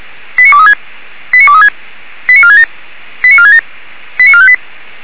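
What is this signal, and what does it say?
Five-tone selective call (ZVEI), heard as quick runs of five tones, tones of about 80 ms. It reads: 89236 89236 89467 89467 89458
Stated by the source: ZVEI five-tone selective call received on SDR.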